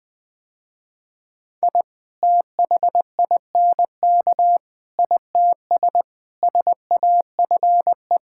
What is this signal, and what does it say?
Morse code, heard as a keyed tone, about 20 wpm, 700 Hz. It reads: I THINK ITS SAFE
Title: Sentences with 6-Word Phrases - Spoken Beginning - 20wpm